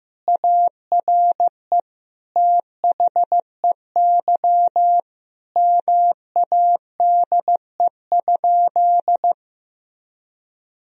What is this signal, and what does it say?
Morse code, heard as a keyed tone, about 15 wpm, 700 Hz. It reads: ARE THEY MADE?
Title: Sentences from Top 1000 Words - Review Part 3 - 15wpm